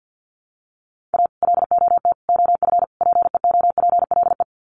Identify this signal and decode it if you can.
Morse code, heard as a keyed tone, about 50 wpm, 700 Hz. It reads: A LOT OF PEOPLE
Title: Sets of 4 Words 50wpm